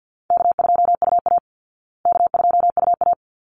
Morse code, heard as a keyed tone, about 50 wpm, 700 Hz. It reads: K2UA K2UA
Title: Call Signs Repeated 3x - 50wpm